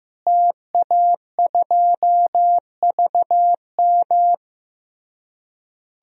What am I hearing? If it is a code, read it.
Morse code, heard as a keyed tone, about 15 wpm, 700 Hz. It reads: TA2VM
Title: DXCC-100 Call Signs - 5 Characters - Repeated 3x - 15wpm